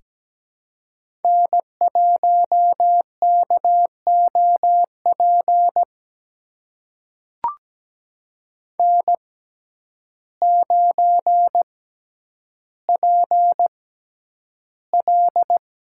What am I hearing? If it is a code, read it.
Morse code, heard as a keyed tone, about 17 wpm, spaced as for 9 wpm, 700 Hz. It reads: N1KOP N9PL
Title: USA Call Signs - 6x Character Spacing - 17wpm